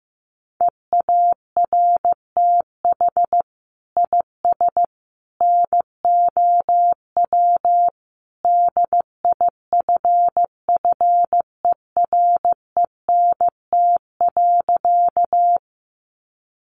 Morse code, heard as a keyed tone, about 15 wpm, 700 Hz. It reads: EARTH IS NOW DIFFERENT.
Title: Sentences from Top 600 Words 15wpm